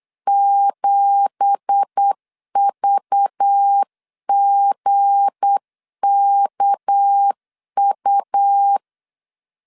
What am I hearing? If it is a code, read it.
Morse code, heard as a keyed tone, about 8 wpm, 800 Hz. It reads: 7VGKU
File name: CW / DET / 1 9 1500C S A 8wpm 5637